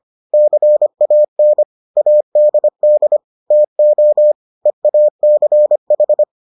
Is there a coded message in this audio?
Yes: Morse code from 0.3 to 6.2 s, heard as a keyed tone, about 25 wpm, 600 Hz. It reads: CAN ADD TO EACH